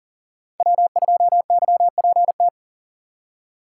Morse code, heard as a keyed tone, about 40 wpm, 700 Hz. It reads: W2YPT
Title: USA Call Signs - 5 Characters - 2x Character Spacing - 40wpm